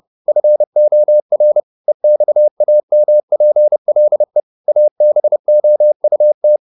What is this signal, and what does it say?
Morse code, heard as a keyed tone, about 30 wpm, 600 Hz. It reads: FOR EXAMPLE ABOUT